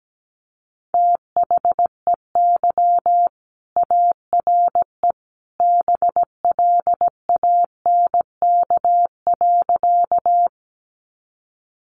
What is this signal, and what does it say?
Morse code, heard as a keyed tone, about 17 wpm, 700 Hz. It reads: THEY ARE BLANK.
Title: Sentences from Top 1500 Words 17wpm